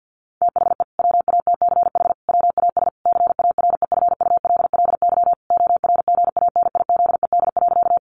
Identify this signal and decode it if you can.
Morse code, heard as a keyed tone, about 50 wpm, 700 Hz. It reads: THE WATCH WAS CAREFULLY ORGANIZED.
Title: Sentences from Top 1300 Words 50wpm